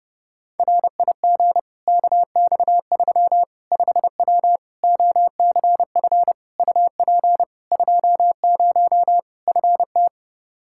Morse code, heard as a keyed tone, about 30 wpm, 700 Hz. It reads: RIG KX3 5W OCF UP 20 FT
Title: Practice QSO - Exchange 2 - Info - 30wpm